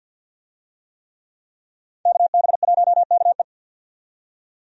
Morse code, heard as a keyed tone, about 52 wpm, 700 Hz. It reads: KB1KE
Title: USA Call Signs - 5 Characters - 50wpm